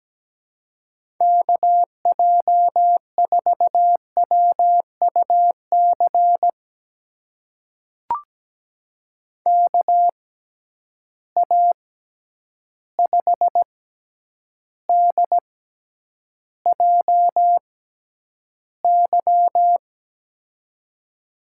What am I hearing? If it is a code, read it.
Morse code, heard as a keyed tone, about 17 wpm, spaced as for 9 wpm, 700 Hz. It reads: KJ4WUC KA5DJY
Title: USA Call Signs - 6 Characters - 6x Character Spacing - 17wpm